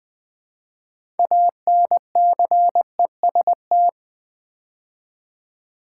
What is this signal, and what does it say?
Morse code, heard as a keyed tone, about 20 wpm, 700 Hz. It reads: ANCEST
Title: Build A Word - 2 Letters At A Time - 8-Letter Words - 20wpm